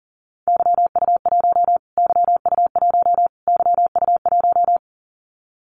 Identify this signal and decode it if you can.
Morse code, heard as a keyed tone, about 40 wpm, 700 Hz. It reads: YU1 YU1 YU1
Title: Callsign Prefixes for Top 25 DXCC - Repeated 3x - 40wpm